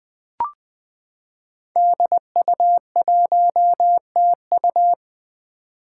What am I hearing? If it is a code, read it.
Morse code, heard as a keyed tone, about 20 wpm, 700 Hz. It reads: DU1TU
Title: Five Challenging Letter-Number Combinations 20wpm